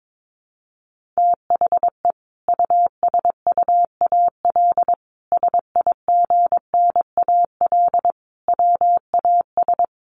Morse code, heard as a keyed tone, about 22 wpm, 700 Hz. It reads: THE USUAL SIGNAL WAS
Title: Sentences from Top 1400 Words 22wpm